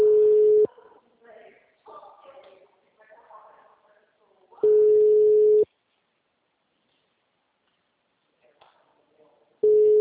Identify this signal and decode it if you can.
Ringback tone (Europe, 425 Hz), with speech.